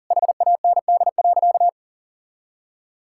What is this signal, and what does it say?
Morse code, heard as a keyed tone, about 40 wpm, 700 Hz. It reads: HAND.
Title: Sentences from Top 1000 Words - Review Part 5 - 40wpm